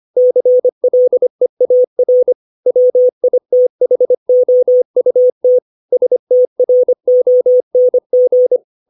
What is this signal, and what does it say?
Morse code, heard as a keyed tone, about 25 wpm, 500 Hz. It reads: CLEAR WITHOUT STRONG